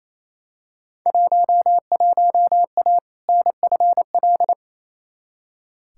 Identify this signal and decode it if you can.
Morse code, heard as a keyed tone, about 28 wpm, 700 Hz. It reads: 11A NFL
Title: Field Day Exchanges Encore - 28wpm